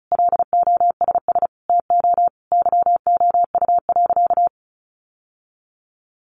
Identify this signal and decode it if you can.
Morse code, heard as a keyed tone, about 34 wpm, 700 Hz. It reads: LOSS TO YOU.
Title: Sentences from Top 1400 Words 35wpm